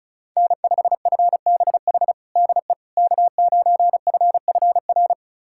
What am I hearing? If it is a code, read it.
Morse code, heard as a keyed tone, about 36 wpm, 700 Hz. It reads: N5FBH DE K9FFR
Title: USA Callsign DE Callsign - 5 Characters - 35wpm